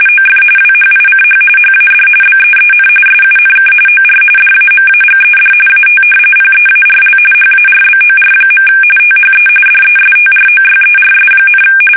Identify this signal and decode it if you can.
RTTY, about 45.45 baud. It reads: OF WHICH HAD BEEN SUNK OR DAMAGED AT PEARL HARBOR. THE DESTROYERS,
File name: fsk850 / battleshipnj